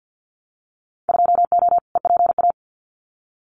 Morse code, heard as a keyed tone, about 50 wpm, 700 Hz. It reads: 2O EPA
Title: Winter Field Day Exchanges Encore 50wpm